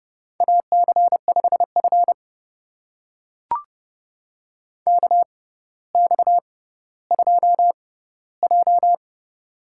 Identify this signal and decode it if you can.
Morse code, heard as a keyed tone, about 31 wpm, spaced as for 16 wpm, 690 Hz. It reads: AC5F KX2J
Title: USA Call Signs - 4 Characters - 6x Character Spacing - 30wpm